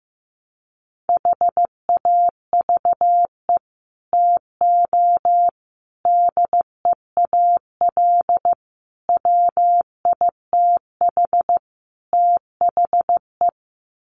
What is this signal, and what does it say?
Morse code, heard as a keyed tone, about 15 wpm, 700 Hz. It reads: HAVE TO DEAL WITH THE
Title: Sets of 5 Words Encore2 15wpm